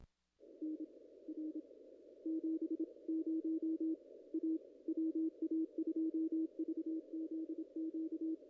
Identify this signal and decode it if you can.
Morse code, heard as a keyed tone, about 27 wpm, spaced as for 23 wpm, 320 Hz. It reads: NR 70 A WA2VZQ
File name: contests / ss-cw-k5tr-op-n5rz -0000